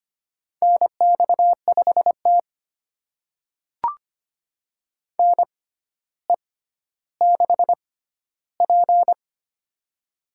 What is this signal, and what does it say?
Morse code, heard as a keyed tone, about 25 wpm, spaced as for 13 wpm, 700 Hz. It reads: NX5T NE6P
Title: USA Call Signs - 4 Characters - 6x Character Spacing - 25wpm